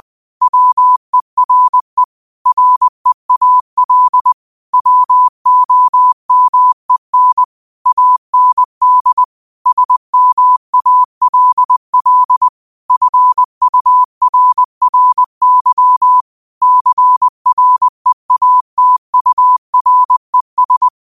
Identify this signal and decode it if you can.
Morse code, heard as a keyed tone, about 20 wpm, 1000 Hz. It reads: WERE REAL WOMEN AND SMALL FURRY CREATURES
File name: qotdfiles / 20wpm